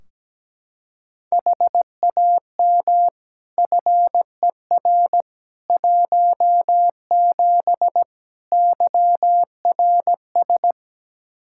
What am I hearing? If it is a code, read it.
Morse code, heard as a keyed tone, about 17 wpm, 700 Hz. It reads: HAM FER 17 YRS
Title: Practice QSO - Exchange 3 - More Info - 17wpm